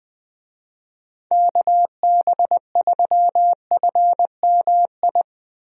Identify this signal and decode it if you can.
Morse code, heard as a keyed tone, about 20 wpm, 700 Hz. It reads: KB3FMI
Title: USA Call Signs - 6 Characters - Repeated 2x - 20wpm